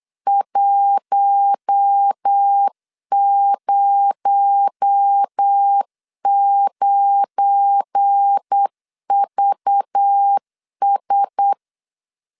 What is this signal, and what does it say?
Morse code, heard as a keyed tone, about 8 wpm, 800 Hz. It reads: 109VS
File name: LEZIONI / 7 GF9Z / 7 GF9 8wpm 3545